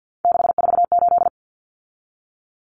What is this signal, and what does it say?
Morse code, heard as a keyed tone, about 48 wpm, 700 Hz. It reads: 648
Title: Numbers - 3-Digit 50wpm